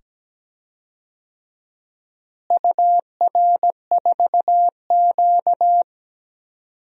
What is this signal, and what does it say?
Morse code, heard as a keyed tone, about 17 wpm, 700 Hz. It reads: UR4Q
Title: DXCC-100 Call Signs - 4 Characters - Repeated 2x - 17wpm